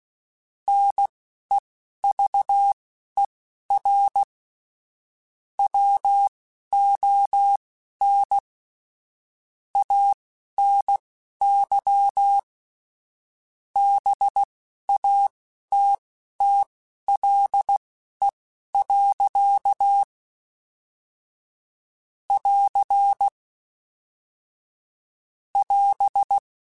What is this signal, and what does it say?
Morse code, heard as a keyed tone, about 16 wpm, spaced as for 12 wpm, 800 Hz. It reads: NEVER WON ANY BATTLE. <AR> <AS>
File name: Quotes / Quotes-mp3s / Pessimism @10